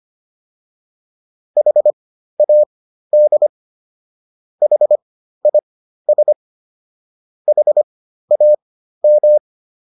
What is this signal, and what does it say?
Morse code, heard as a keyed tone, about 25 wpm, spaced as for 13 wpm, 600 Hz. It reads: HAD HIS HAM